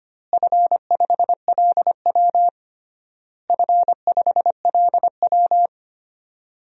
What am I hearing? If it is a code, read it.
Morse code, heard as a keyed tone, about 25 wpm, 700 Hz. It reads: F5LW F5LW
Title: DXCC-100 Call Signs - 4 Characters - Repeated 3x - 25wpm